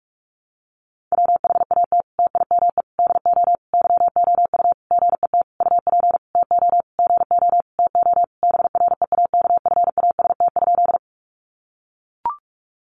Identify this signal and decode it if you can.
Morse code, heard as a keyed tone, about 44 wpm, 700 Hz. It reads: WHAT TIME DO YOU GET UP TO GO TO BREAKFAST?
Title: Sentences from Top 1500 Words 45wpm